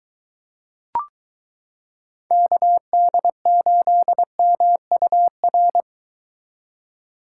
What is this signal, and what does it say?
Morse code, heard as a keyed tone, about 23 wpm, 700 Hz. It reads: KD8MUR